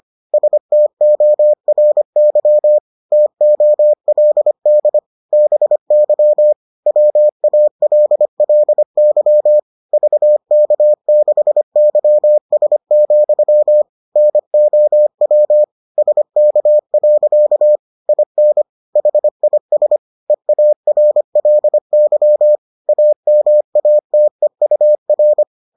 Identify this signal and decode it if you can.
Morse code, heard as a keyed tone, about 25 wpm, 600 Hz. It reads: STORY TOLD BY WALLY VK6YS, NOW SK. IN HIS EARLY AMATEUR